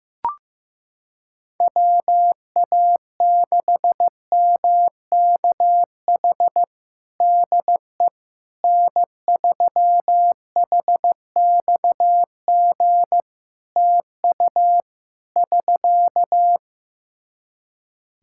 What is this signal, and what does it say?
Morse code, heard as a keyed tone, about 15 wpm, 700 Hz. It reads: WA6MKH DE N3HXG TU <SK>
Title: Practice QSO - Exchange 4 - Ending - 15wpm